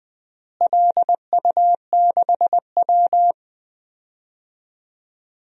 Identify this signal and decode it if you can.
Morse code, heard as a keyed tone, about 20 wpm, 700 Hz. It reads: LU6W
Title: DXCC-100 Call Signs - 4 Characters - Repeated 3x - 20wpm